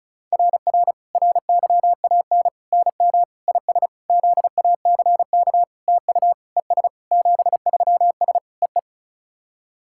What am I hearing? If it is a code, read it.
Morse code, heard as a keyed tone, about 36 wpm, 690 Hz. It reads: RR RYAN NM IS ZACK TU ES 73S EE
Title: SOTA QSO Practice 35wpm